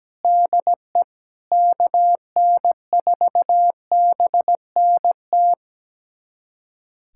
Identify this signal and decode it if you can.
Morse code, heard as a keyed tone, about 17 wpm, 700 Hz. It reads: DE KN4BNT
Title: USA Callsign DE Callsign - 6 Characters - 17wpm